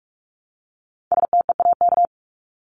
Morse code, heard as a keyed tone, about 44 wpm, 700 Hz. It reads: STEAK
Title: Top 1251-1500 Words 45wpm